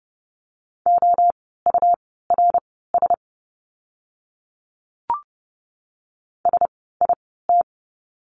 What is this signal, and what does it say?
Morse code, heard as a keyed tone, about 30 wpm, spaced as for 17 wpm, 700 Hz. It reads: OURS SIT
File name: Lesson 17f - Letter u - Words 30wpm